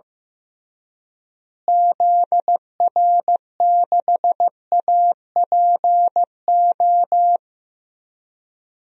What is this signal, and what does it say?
Morse code, heard as a keyed tone, about 15 wpm, 700 Hz. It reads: ZR6APO
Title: DXCC-100 Call Signs - 6 Characters - Repeated 3x - 15wpm